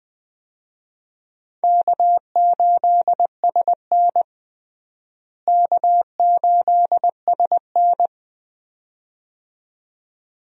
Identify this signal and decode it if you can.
Morse code, heard as a keyed tone, about 20 wpm, 700 Hz. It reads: K8SN K8SN
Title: USA Call Signs - 4 Characters - Repeated 2x - 20wpm